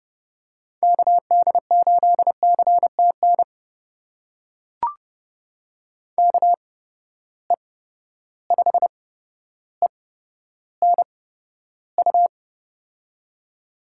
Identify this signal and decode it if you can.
Morse code, heard as a keyed tone, about 30 wpm, spaced as for 13 wpm, 700 Hz. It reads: KD8CTN KE5ENU
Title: USA Call Signs - 6 Characters - 8x Character Spacing - 30wpm